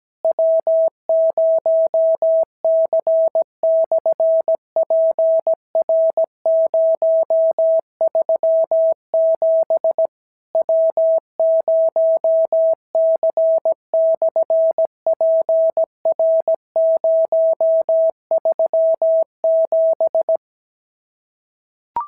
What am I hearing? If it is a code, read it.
Morse code, heard as a keyed tone, about 17 wpm, 650 Hz. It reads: W0C/PR037 W0C/PR037
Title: SOTA QSO Practice 17wpm